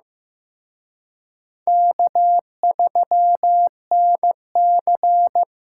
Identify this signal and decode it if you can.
Morse code, heard as a keyed tone, about 18 wpm, 700 Hz. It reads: K3NC
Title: Speed Racing - USA 1x2 Call Signs 15wpm